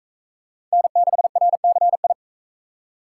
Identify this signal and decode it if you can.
Morse code, heard as a keyed tone, about 45 wpm, 700 Hz. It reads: N6RCI